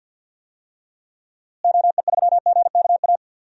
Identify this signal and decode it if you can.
Morse code, heard as a keyed tone, about 53 wpm, 700 Hz. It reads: OE3GKA
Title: DXCC-100 Call Signs - 6 Characters - Repeated 2x - 50wpm